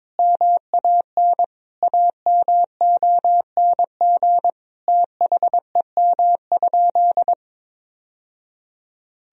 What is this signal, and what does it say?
Morse code, heard as a keyed tone, about 22 wpm, 700 Hz. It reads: MAN AMONG THEM?
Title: Sentences from Top 1000 Words - Review Part 3 - 22wpm